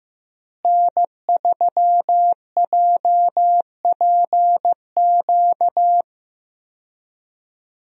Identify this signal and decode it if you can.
Morse code, heard as a keyed tone, about 15 wpm, 700 Hz. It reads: N3JPQ
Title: USA Call Signs - 5 Characters - Repeated 2x - 15wpm